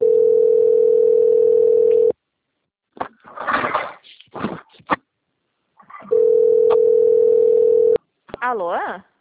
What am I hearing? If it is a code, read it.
Ringback tone (North America), 440 + 480 Hz, with speech.
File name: out- - - - - .18510